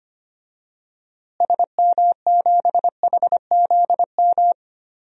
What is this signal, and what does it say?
Morse code, heard as a keyed tone, about 25 wpm, 700 Hz. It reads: SM7HZM
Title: DXCC-100 Call Signs - 6 Characters - Repeated 3x - 25wpm